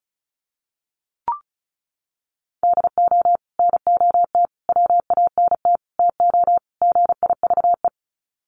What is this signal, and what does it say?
Morse code, heard as a keyed tone, about 34 wpm, 700 Hz. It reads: DO NOT WANT TO GIVE
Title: Sets of 5 Words Encore 35wpm